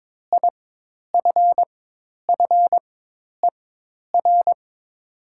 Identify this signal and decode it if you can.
Morse code, heard as a keyed tone, about 22 wpm, spaced as for 10 wpm, 700 Hz. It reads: IFFER
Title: 4x Character Spacing - Top 300 Words - 22wpm